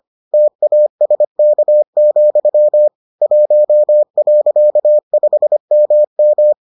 Morse code, heard as a keyed tone, about 25 wpm, 600 Hz. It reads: TASK, 1.5MM